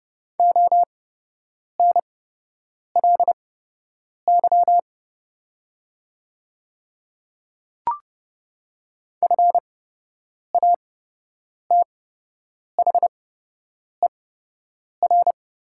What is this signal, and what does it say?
Morse code, heard as a keyed tone, about 29 wpm, spaced as for 8 wpm, 690 Hz. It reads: ONLY FATHER
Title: 8x Character Spacing - Top 200 Words - 30wpm